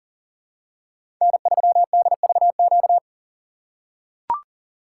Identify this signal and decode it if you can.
Morse code, heard as a keyed tone, about 41 wpm, 700 Hz. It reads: N3DVQ
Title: USA Call Signs - 3x Character Spacing - 40wpm